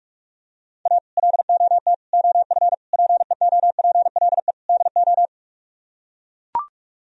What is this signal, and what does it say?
Morse code, heard as a keyed tone, about 44 wpm, 700 Hz. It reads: A LOT OF PEOPLE DO